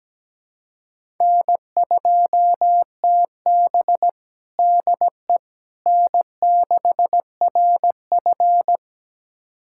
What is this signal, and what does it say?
Morse code, heard as a keyed tone, about 17 wpm, 700 Hz. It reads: N2TB DE N6RF
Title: USA Callsign DE Callsign - 4 Characters - 17wpm